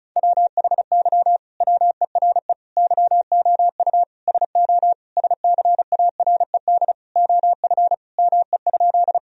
Morse code, heard as a keyed tone, about 34 wpm, 700 Hz. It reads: WHY WERE YOU SO SCARED OF ME?
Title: Sentences from Top 1000 Words - Review Part 4 - 35wpm